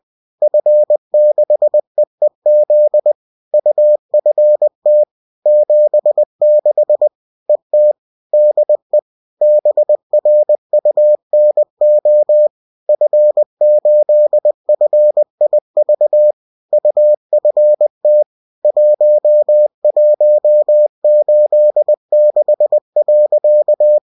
Morse code, heard as a keyed tone, about 20 wpm, 600 Hz. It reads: F6EEZ UFT 76 ET DE BRUNO F8FIV UFT 1186.